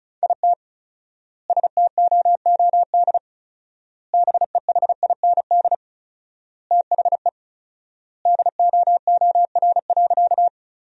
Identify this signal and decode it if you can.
Morse code, heard as a keyed tone, about 34 wpm, 700 Hz. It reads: IT STOOD BEHIND THE DOOR.
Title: 4x Word-Spacing - Sentences from Top 500 Words - 35wpm